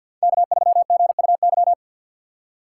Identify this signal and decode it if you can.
Morse code, heard as a keyed tone, about 53 wpm, 700 Hz. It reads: K3GUY
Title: USA Call Signs - 6x Character Spacing - 50wpm